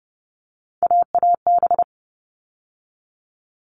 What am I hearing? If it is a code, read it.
Morse code, heard as a keyed tone, about 30 wpm, 700 Hz. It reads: AAB
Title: All Cut Numbers - 3-Digit 30wpm